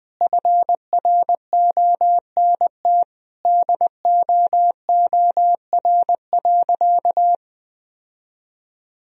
Morse code, heard as a keyed tone, about 20 wpm, 700 Hz. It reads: FRONT DOOR.
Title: Sentences from Top 1000 Words - Review Part 2 - 20wpm